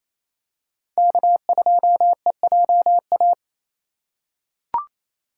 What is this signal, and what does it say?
Morse code, heard as a keyed tone, about 28 wpm, 700 Hz. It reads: K2EJA
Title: USA Call Signs - 5 Characters - 2x Character Spacing - 28wpm